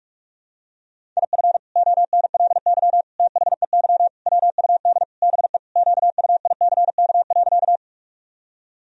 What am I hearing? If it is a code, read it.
Morse code, heard as a keyed tone, about 46 wpm, 700 Hz. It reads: IF ONLY THEY WUD BE QUICK.
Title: Sentences from Top 500 Words and Abbreviations 45wpm